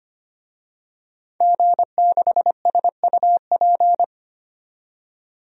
Morse code, heard as a keyed tone, about 25 wpm, 700 Hz. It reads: G6SUP
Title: DXCC-100 Call Signs - 5 Characters - 25wpm